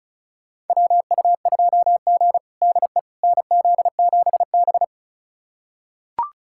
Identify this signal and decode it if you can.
Morse code, heard as a keyed tone, about 36 wpm, 700 Hz. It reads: WU2G DE NZ7B
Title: USA Callsign DE Callsign - 4 Characters - 35wpm